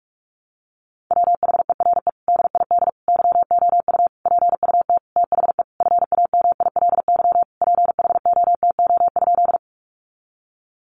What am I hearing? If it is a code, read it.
Morse code, heard as a keyed tone, about 44 wpm, 700 Hz. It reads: WHERE DID YOU PUT THE FAMILY PHOTO?
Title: Sentences from Top 1200 Words 45wpm